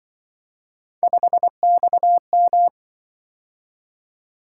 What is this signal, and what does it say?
Morse code, heard as a keyed tone, about 25 wpm, 700 Hz. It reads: 5XM
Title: Speed-Racing - Three Letter-Number Combinations - 20wpm